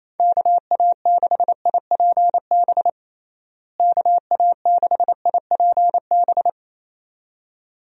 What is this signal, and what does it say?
Morse code, heard as a keyed tone, about 28 wpm, 700 Hz. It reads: KA6IPB KA6IPB
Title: USA Call Signs - 6 Characters - Repeated 2x - 28wpm